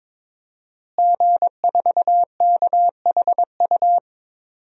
Morse code, heard as a keyed tone, about 22 wpm, 700 Hz. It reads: G4KHU
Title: DXCC-100 Call Signs - 5 Characters - Repeated 3x - 22wpm